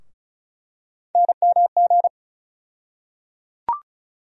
Morse code, heard as a keyed tone, about 36 wpm, 700 Hz. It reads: NMG
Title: USA Call Sign Postfixes - 3 Letters 35wpm